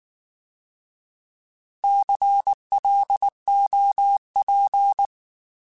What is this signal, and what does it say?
Morse code, heard as a keyed tone, about 19 wpm, 800 Hz. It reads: CLOP
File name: Alphabetical / Alphabetical mp3s / Cs @18